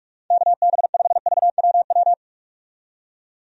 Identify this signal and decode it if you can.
Morse code, heard as a keyed tone, about 41 wpm, 700 Hz. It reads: KB5VWW
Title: USA Call Signs - 6 Characters - 45wpm